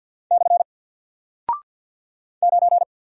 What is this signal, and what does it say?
Morse code, heard as a keyed tone, about 50 wpm, 700 Hz. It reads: / 9
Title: Single Character (Letter:Number:Punctuation:Prosign) Alt 50wpm